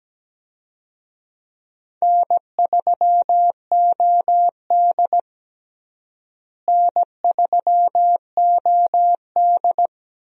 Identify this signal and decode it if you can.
Morse code, heard as a keyed tone, about 17 wpm, 700 Hz. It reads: N3OD N3OD
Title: USA Call Signs - 4 Characters - Repeated 2x - 17wpm